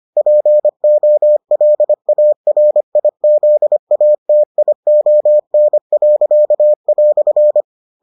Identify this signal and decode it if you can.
Morse code, heard as a keyed tone, about 25 wpm, 600 Hz. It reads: POLARIZATION."